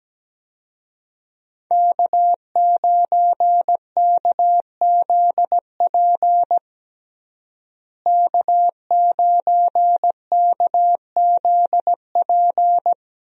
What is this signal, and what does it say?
Morse code, heard as a keyed tone, about 17 wpm, 700 Hz. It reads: K9KZP K9KZP
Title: USA Call Signs - 5 Characters - Repeated 3x - 17wpm